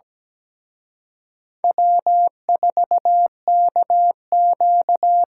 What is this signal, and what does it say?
Morse code, heard as a keyed tone, about 20 wpm, 700 Hz. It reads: W4KQ
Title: Speed Racing - USA 1x2 Call Signs 17wpm